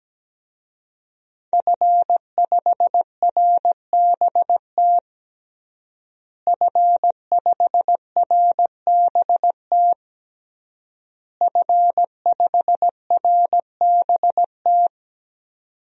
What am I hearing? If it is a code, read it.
Morse code, heard as a keyed tone, about 17 wpm, 700 Hz. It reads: F5RBT F5RBT F5RBT
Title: DXCC-100 Call Signs - 5 Characters - Repeated 3x - 17wpm